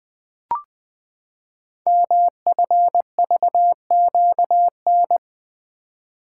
Morse code, heard as a keyed tone, about 20 wpm, 700 Hz. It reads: MFVQN
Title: Code Groups - 5 Letters - 20wpm